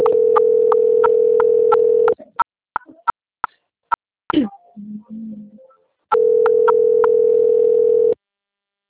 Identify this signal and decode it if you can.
Ringback tone (North America), 440 + 480 Hz.